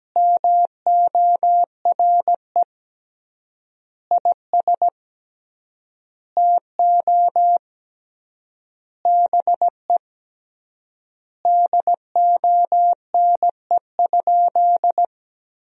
Morse code, heard as a keyed tone, about 17 wpm, 700 Hz. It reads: MORE IS TO BE DONE?
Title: 3x Word-Spacing - Sentences from Top 500 Words - 17wpm